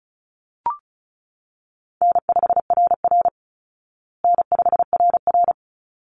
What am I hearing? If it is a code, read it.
Morse code, heard as a keyed tone, about 34 wpm, 700 Hz. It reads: N5RR N5RR
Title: Easy Call Signs Repeated 2x - 35wpm